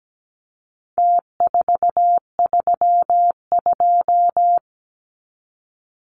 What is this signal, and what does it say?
Morse code, heard as a keyed tone, about 17 wpm, 700 Hz. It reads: T432
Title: Numbers - Contest Style - 4-Digit 17wpm